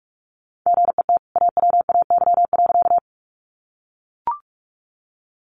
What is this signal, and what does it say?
Morse code, heard as a keyed tone, about 44 wpm, 700 Hz. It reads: GET AWAY.
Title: Sentences from Top 600 Words 45wpm